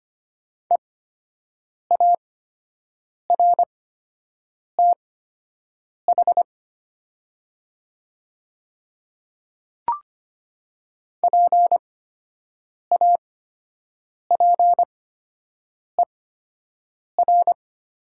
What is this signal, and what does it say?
Morse code, heard as a keyed tone, about 25 wpm, spaced as for 7 wpm, 700 Hz. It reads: EARTH PAPER
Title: 8x Character Spacing - Top 300 Words - 25wpm